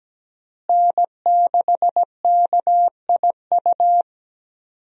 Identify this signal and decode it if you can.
Morse code, heard as a keyed tone, about 17 wpm, 700 Hz. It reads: N6KIU
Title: USA Call Signs - 5 Characters - Repeated 3x - 17wpm